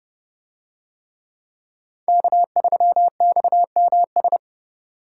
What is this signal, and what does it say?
Morse code, heard as a keyed tone, about 29 wpm, 700 Hz. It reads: K3XMS